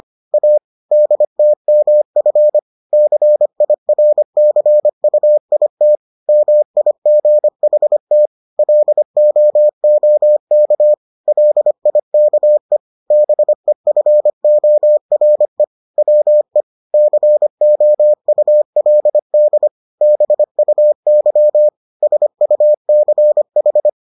Morse code, heard as a keyed tone, about 25 wpm, 600 Hz. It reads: A DTMF CIRCUIT MIGHT LOOK LIKE BEFORE WE COULD BUY SUCH